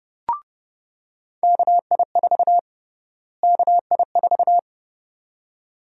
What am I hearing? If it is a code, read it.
Morse code, heard as a keyed tone, about 30 wpm, 700 Hz. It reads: KI4 KI4
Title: USA Call Sign Prefixes - Repeated 2x - 30wpm